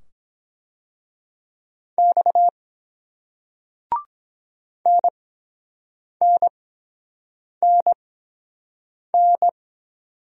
Morse code, heard as a keyed tone, about 23 wpm, spaced as for 7 wpm, 700 Hz. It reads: X NNNN